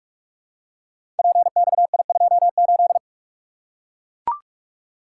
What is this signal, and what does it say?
Morse code, heard as a keyed tone, about 44 wpm, 690 Hz. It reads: PXI28
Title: Code Groups - 5 Letter:Numbers - 45wpm